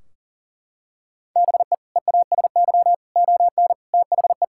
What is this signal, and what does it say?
Morse code, heard as a keyed tone, about 40 wpm, 700 Hz. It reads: BE EASY ON THE
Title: Sentences from Top 1600 Words 40wpm